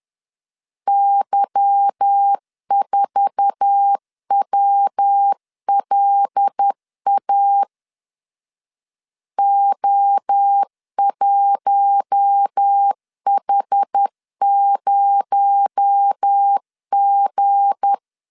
Morse code, heard as a keyed tone, about 10 wpm, 800 Hz. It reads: Y4WLA O1H0G
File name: LEZIONI / 8 YJ45 / 8 YJ4 10wpm 2633